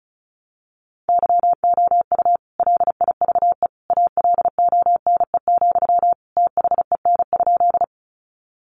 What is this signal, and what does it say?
Morse code, heard as a keyed tone, about 34 wpm, 700 Hz. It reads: YOU LIVE ALONE, THEN?
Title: Sentences from Top 1000 Words 35wpm